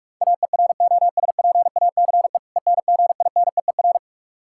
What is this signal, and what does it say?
Morse code, heard as a keyed tone, about 46 wpm, 700 Hz. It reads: AEROSPACE ENGINEER
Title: Practice QSO - Exchange 3 - More Info - 45wpm